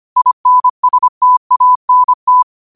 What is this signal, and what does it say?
Morse code, heard as a keyed tone, about 25 wpm, 1000 Hz. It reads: INSTANT
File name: qotdfiles / 25wpm